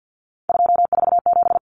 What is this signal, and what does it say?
Morse code, heard as a keyed tone, about 50 wpm, 700 Hz. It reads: 247
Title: Numbers - 3-Digit 50wpm